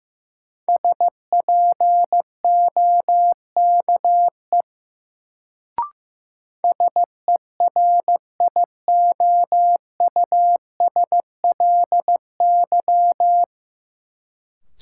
Morse code, heard as a keyed tone, about 15 wpm, 700 Hz. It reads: SPOKE SERIOUSLY
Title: Sentences from Top 1200 Words - New Word Drill - 15wpm